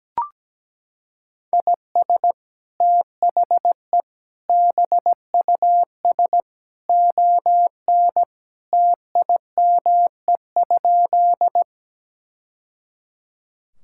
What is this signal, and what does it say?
Morse code, heard as a keyed tone, about 17 wpm, 700 Hz. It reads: IS THE BUS ON TIME?
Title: Sentences from Top 1000 Words - Review Part 6 - 17wpm